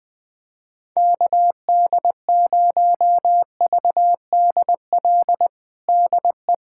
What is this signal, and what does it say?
Morse code, heard as a keyed tone, about 20 wpm, 700 Hz. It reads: KD0VDL DE
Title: USA Callsign DE Callsign - 6 Characters - 20wpm